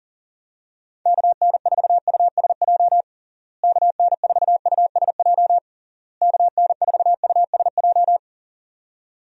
Morse code, heard as a keyed tone, about 40 wpm, 700 Hz. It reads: KN4USJ KN4USJ KN4USJ
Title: USA Call Signs - 6 Characters - Repeated 3x - 40wpm